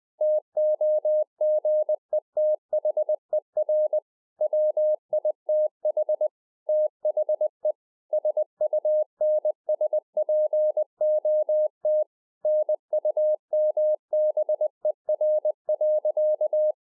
Morse code, heard as a keyed tone, about 20 wpm, 610 Hz. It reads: TOGETHER WITH THE SUNSPOT NUMBER.
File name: 20WPM